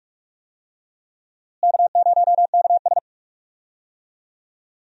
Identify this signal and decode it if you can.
Morse code, heard as a keyed tone, about 46 wpm, 700 Hz. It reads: K0KS
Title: USA Call Signs - 4 Characters - Repeated 2x - 45wpm